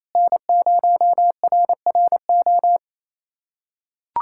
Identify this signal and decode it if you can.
Morse code, heard as a keyed tone, about 28 wpm, 700 Hz. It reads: N0RRO
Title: USA Callsign DE Callsign 28wpm